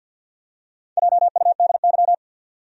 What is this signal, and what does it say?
Morse code, heard as a keyed tone, about 46 wpm, 700 Hz. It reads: JUDY
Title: USA Common Names - Max 4 Letters - 50wpm